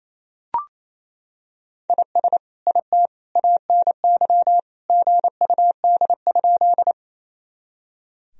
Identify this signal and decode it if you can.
Morse code, heard as a keyed tone, about 28 wpm, 700 Hz. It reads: IS IT ANY GUD?
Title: Sentences from Top 200 Words and Abbreviations 28wpm